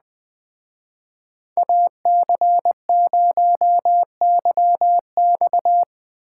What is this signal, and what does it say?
Morse code, heard as a keyed tone, about 20 wpm, 700 Hz. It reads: AC0YX
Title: Speed Racing - USA 2x2 Call Signs 17wpm